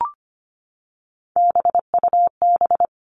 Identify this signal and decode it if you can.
Morse code, heard as a keyed tone, about 25 wpm, 700 Hz. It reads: BUB
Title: All Cut Numbers - 3-Digit 25wpm